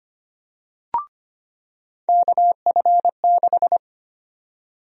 Morse code, heard as a keyed tone, about 25 wpm, 700 Hz. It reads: KF6
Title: USA Call Sign Prefixes - 25wpm